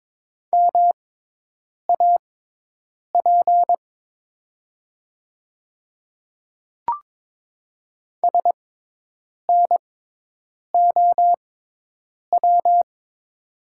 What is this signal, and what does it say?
Morse code, heard as a keyed tone, about 22 wpm, spaced as for 8 wpm, 700 Hz. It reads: MAP SNOW
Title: 6x Character Spacing - Top 500 Words - 22wpm